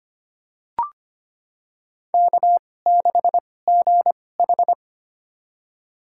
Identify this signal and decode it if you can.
Morse code, heard as a keyed tone, about 25 wpm, spaced as for 18 wpm, 700 Hz. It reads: K6GH
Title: USA Call Signs - 4 Characters - 2x Character Spacing - 25wpm